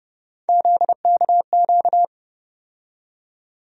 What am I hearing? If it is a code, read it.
Morse code, heard as a keyed tone, about 32 wpm, 700 Hz. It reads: ZKQ
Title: USA Call Sign Postfixes - 3 Letters 30wpm